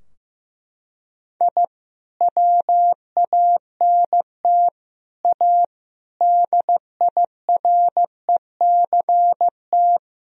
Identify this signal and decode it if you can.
Morse code, heard as a keyed tone, about 15 wpm, 700 Hz. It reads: I WANT A DIRECT
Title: Sentences from Top 1000 Words - Review Part 2 - 15wpm